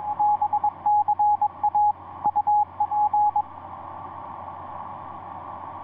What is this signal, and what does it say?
Morse code, heard as a keyed tone, about 21 wpm, 840 Hz, with radio noise. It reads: BCA UP
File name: ce1-k7ca ea8tl etc 17m cw 10jan2010 1620